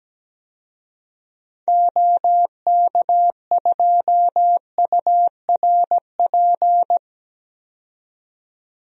Morse code, heard as a keyed tone, about 17 wpm, 700 Hz. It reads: OK2URP
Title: DXCC-100 Call Signs - 6 Characters - 17wpm